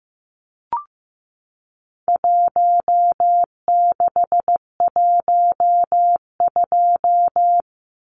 Morse code, heard as a keyed tone, about 15 wpm, 700 Hz. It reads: 1612
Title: Numbers - Contest Style - 4-Digit 15wpm